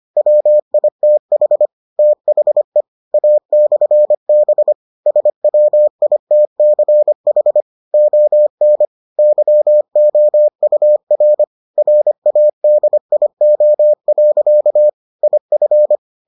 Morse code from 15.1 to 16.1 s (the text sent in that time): IF